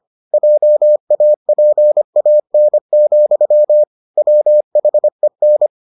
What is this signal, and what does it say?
Morse code, heard as a keyed tone, about 25 wpm, 600 Hz. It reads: JAPAN, WHEN